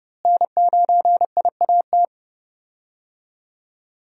Morse code, heard as a keyed tone, about 30 wpm, 700 Hz. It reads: N9IAT
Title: USA Call Signs - 5 Characters - Repeated 2x - 30wpm